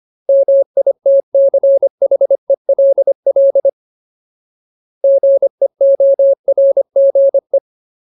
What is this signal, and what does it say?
Morse code, heard as a keyed tone, about 25 wpm, 550 Hz. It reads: MITCHELL GEORGE
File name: Ultimate Word List / 25WPM Ultimate